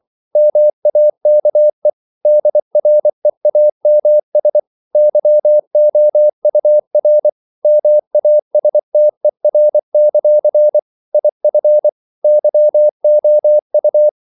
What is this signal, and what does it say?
Morse code, heard as a keyed tone, about 24 wpm, 600 Hz. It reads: MAKE DREAMS YOUR MASTER; IF YOU
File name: CW / Practice / If24